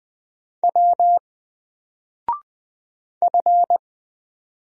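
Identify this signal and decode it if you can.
Morse code, heard as a keyed tone, about 20 wpm, 700 Hz. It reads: W F
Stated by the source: Single letter and number Morse drill at 20 wpm, from N4LCD.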